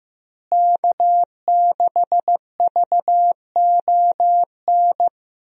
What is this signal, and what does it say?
Morse code, heard as a keyed tone, about 15 wpm, 700 Hz. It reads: K6VON
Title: USA Call Signs - 5 Characters - Repeated 3x - 15wpm